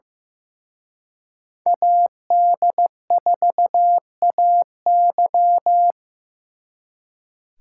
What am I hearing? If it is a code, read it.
Morse code, heard as a keyed tone, about 18 wpm, 700 Hz. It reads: AD4AY
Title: Speed Racing - USA 2x2 Call Signs 15wpm